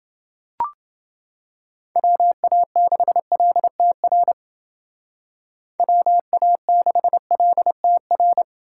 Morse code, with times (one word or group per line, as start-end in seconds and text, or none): 2.0-4.3 WA6LTR
5.8-8.4 WA6LTR